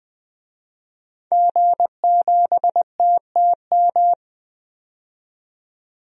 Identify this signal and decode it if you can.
Morse code, heard as a keyed tone, about 20 wpm, 700 Hz. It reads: G7TTM
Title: DXCC-100 Call Signs - 5 Characters - 20wpm